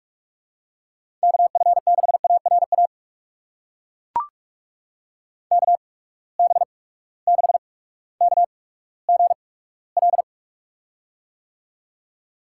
Morse code, heard as a keyed tone, about 42 wpm, spaced as for 19 wpm, 700 Hz. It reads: KF6ARA KB6KGL
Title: USA Call Signs - 6 Characters - 8x Character Spacing - 45wpm